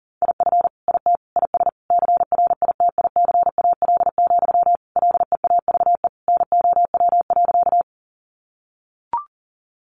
Morse code, heard as a keyed tone, about 40 wpm, 700 Hz. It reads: IF IT IS CRITICAL, LEAVE NOW.
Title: Sentences from Top 1100 Words 40wpm